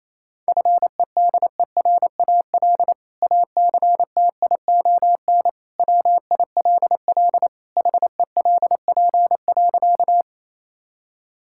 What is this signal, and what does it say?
Morse code, heard as a keyed tone, about 28 wpm, 700 Hz. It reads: FEDERAL ACTION WILL HELP.
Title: Sentences from Top 1000 Words - Review Part 5 - 28wpm